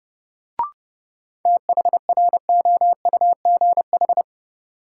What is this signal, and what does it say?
Morse code, heard as a keyed tone, about 30 wpm, 700 Hz. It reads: THROUGH
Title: Sets of 2 Words - New Word Drill - 30wpm